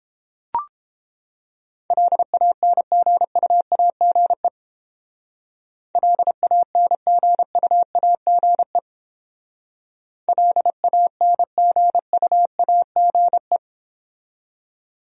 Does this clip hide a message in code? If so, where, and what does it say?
Yes: Morse code from 1.9 to 13.6 s, heard as a keyed tone, about 29 wpm, 700 Hz. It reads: LANGUAGE LANGUAGE LANGUAGE